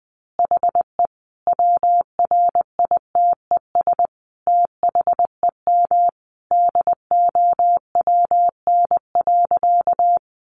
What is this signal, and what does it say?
Morse code, heard as a keyed tone, about 20 wpm, 700 Hz. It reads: HE WRITES THEM DOWN.
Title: Additional Easy Phrases 20wpm